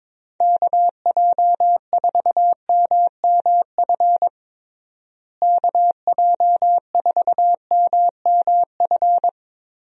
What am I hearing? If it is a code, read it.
Morse code, heard as a keyed tone, about 22 wpm, 700 Hz. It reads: KJ4MMF KJ4MMF
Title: USA Call Signs - 6 Characters - Repeated 3x - 22wpm